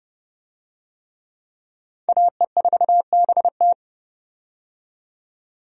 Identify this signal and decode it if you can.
Morse code, heard as a keyed tone, about 32 wpm, 700 Hz. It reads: AE4BT
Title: USA Call Signs - 5 Characters - 30wpm